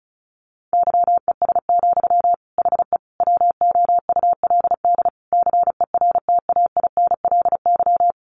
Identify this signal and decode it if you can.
Morse code, heard as a keyed tone, about 34 wpm, 700 Hz. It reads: YES, HE WOULD CERTAINLY
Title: Sentences from Top 900 Words 35wpm